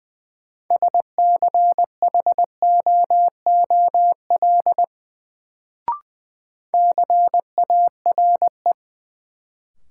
Morse code, heard as a keyed tone, about 20 wpm, 700 Hz. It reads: SCHOOL CARE
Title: Sentences from Top 300 Words - New Word Drill - 20wpm